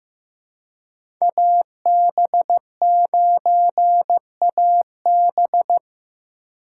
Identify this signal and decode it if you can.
Morse code, heard as a keyed tone, about 15 wpm, 700 Hz. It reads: AB9AB
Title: USA Call Signs - 5 Characters - 3x Character Spacing - 15wpm